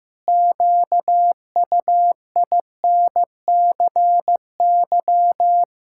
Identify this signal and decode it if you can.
Morse code, heard as a keyed tone, about 14 wpm, 700 Hz. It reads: QUINCY
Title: Pangram Challenge v2 - 15wpm